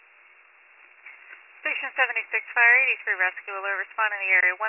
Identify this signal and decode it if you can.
Emergency Alert System (SAME) tones, heard as harsh buzzing bursts, with speech. It is heard from 1.0 to 4.1 s.